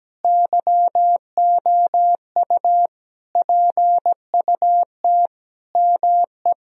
Morse code, heard as a keyed tone, about 17 wpm, 700 Hz. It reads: YOU PUT ME
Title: Sentences from Top 1000 Words - Review Part 3 - 17wpm